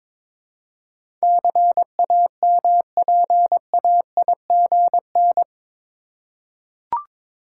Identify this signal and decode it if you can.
Morse code, heard as a keyed tone, about 22 wpm, 700 Hz. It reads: CAMPAIGN
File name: practice / 22wpm / Eight Letter Words - 22wpm